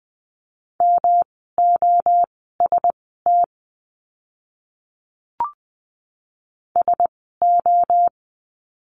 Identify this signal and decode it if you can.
Morse code, heard as a keyed tone, about 20 wpm, spaced as for 14 wpm, 700 Hz. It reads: MOST SO
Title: 108 Lesson 18f - Letter m - Words 20wpm